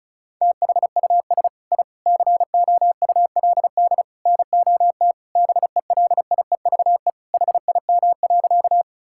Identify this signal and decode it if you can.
Morse code, heard as a keyed tone, about 36 wpm, 700 Hz. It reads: THUS I COULD NOT BELIEVE HIM.
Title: Sentences from Top 1000 Words - Review Part 5 - 35wpm